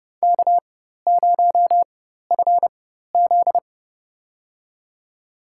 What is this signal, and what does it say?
Morse code, heard as a keyed tone, about 30 wpm, spaced as for 14 wpm, 700 Hz. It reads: K0FZ
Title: USA Call Signs - 4 Characters - 4x Character Spacing - 30wpm